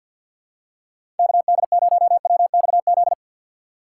Morse code, heard as a keyed tone, about 52 wpm, 700 Hz. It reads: KD0WXZ
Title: USA Call Signs - 6 Characters - 3x Character Spacing - 50wpm